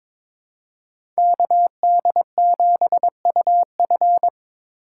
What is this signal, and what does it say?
Morse code, heard as a keyed tone, about 22 wpm, 700 Hz. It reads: KD7UF
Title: USA Call Signs - 5 Characters - 4x Character Spacing - 22wpm